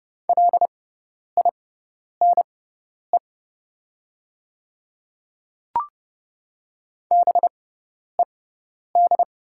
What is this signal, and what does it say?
Morse code, heard as a keyed tone, about 30 wpm, spaced as for 10 wpm, 700 Hz. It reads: LINE BED
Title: 6x Character Spacing - Top 500 Words - 30wpm